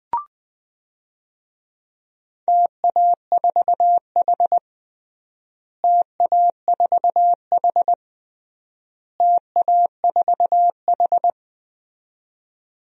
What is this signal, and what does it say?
Morse code, heard as a keyed tone, about 21 wpm, 700 Hz. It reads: TA4H TA4H TA4H
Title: DXCC-100 Call Signs - 4 Characters - Repeated 3x - 20wpm